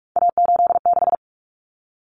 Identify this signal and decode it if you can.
Morse code, heard as a keyed tone, about 44 wpm, 700 Hz. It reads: A86
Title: Common Cut Numbers - 3-Digit 45wpm